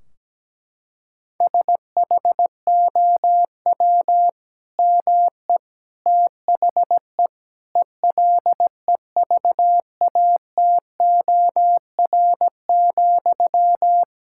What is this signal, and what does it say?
Morse code, heard as a keyed tone, about 17 wpm, 700 Hz. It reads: SHOW ME THE ELEVATOR,
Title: Sentences from Top 1600 Words 17wpm